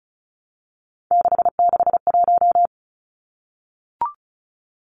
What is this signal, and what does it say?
Morse code, heard as a keyed tone, about 34 wpm, 700 Hz. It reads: 661
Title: Numbers - 3-Digit 35wpm